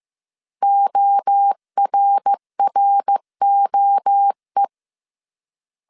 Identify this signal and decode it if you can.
Morse code, heard as a keyed tone, about 15 wpm, 800 Hz. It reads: ORROE